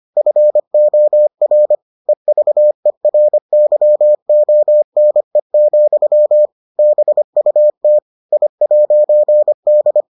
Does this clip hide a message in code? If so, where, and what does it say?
Yes: Morse code from 0.2 to 10.0 s, heard as a keyed tone, about 25 wpm, 600 Hz. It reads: FOR EVERYONE, BUT I'D